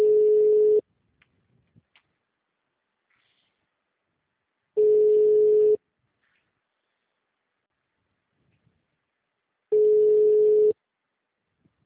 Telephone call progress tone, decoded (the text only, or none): ringback tone